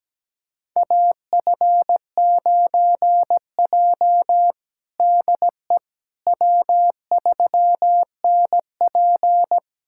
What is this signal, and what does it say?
Morse code, heard as a keyed tone, about 17 wpm, 700 Hz. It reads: AF9J DE W3NP
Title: USA Callsign DE Callsign - 4 Characters - 17wpm